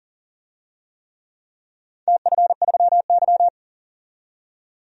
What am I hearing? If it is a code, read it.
Morse code, heard as a keyed tone, about 40 wpm, 700 Hz. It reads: TF3Y
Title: DXCC-100 Call Signs - 4 Characters - 40wpm